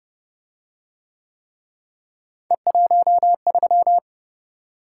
Morse code, heard as a keyed tone, about 28 wpm, 700 Hz. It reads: E13